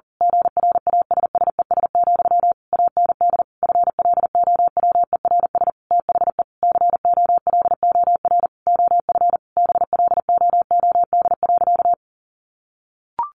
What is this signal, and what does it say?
Morse code, heard as a keyed tone, about 40 wpm, 700 Hz. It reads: GRASSES, AND FLOWERS THE COLOR OF BLOOD.
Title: A Sound Of Thunder 40wpm